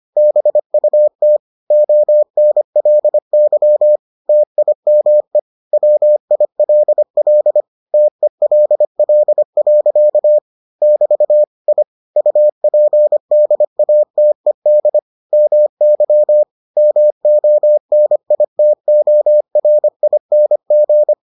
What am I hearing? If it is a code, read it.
Morse code, heard as a keyed tone, about 25 wpm, 600 Hz. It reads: BUT ONLY TIME WILL TELL. <BT> I UPDATED MY MONITORING